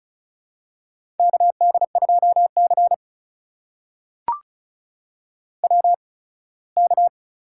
Morse code, heard as a keyed tone, about 36 wpm, spaced as for 21 wpm, 700 Hz. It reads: KD2C WK
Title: USA Call Signs - 4 Characters - 8x Character Spacing - 35wpm